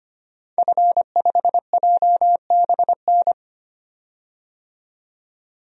Morse code, heard as a keyed tone, about 25 wpm, 700 Hz. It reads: F5JBN
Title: DXCC-100 Call Signs - 5 Characters - 25wpm